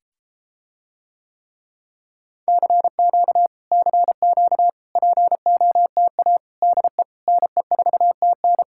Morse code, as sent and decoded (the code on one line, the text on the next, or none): -.-. --.- / -.-. --.- / .--. --- - .- / -.. / -. ....- - -.
CQ CQ POTA DE NE4TN